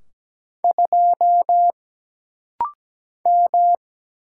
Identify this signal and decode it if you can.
Morse code, heard as a keyed tone, about 17 wpm, 700 Hz. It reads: 2 M
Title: Single Letter-Number - Rapid-Fire 17wpm